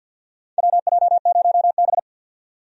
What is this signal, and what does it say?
Morse code, heard as a keyed tone, about 49 wpm, 700 Hz. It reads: WJ0B
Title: USA Call Signs - 4 Characters - 50wpm